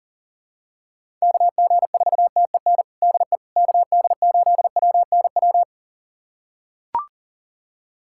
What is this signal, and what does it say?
Morse code, heard as a keyed tone, about 40 wpm, 700 Hz. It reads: KG4TEN DE KD8WNW
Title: USA Callsign DE Callsign - 6 Characters - 40wpm